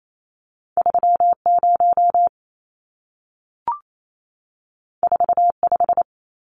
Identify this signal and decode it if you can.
Morse code, heard as a keyed tone, about 28 wpm, 700 Hz. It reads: 30 45
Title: Numbers - 2-Digit 28wpm